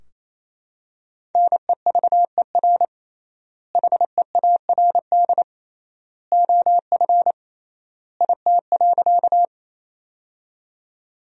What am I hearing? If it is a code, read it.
Morse code, heard as a keyed tone, about 28 wpm, 700 Hz. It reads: NEVER HEARD OF IT.
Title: 3x Word-Spacing - Sentences from Top 400 Words - 28wpm